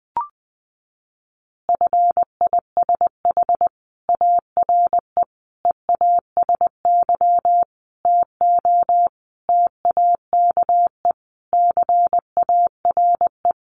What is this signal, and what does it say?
Morse code, heard as a keyed tone, about 20 wpm, 700 Hz. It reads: FISH ARE EASY TO TAKE CARE
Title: Additional Easy Phrases 20wpm